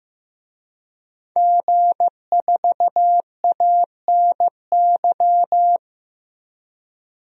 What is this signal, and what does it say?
Morse code, heard as a keyed tone, about 16 wpm, 700 Hz. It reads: G4ANY
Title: DXCC-100 Call Signs - 5 Characters - 15wpm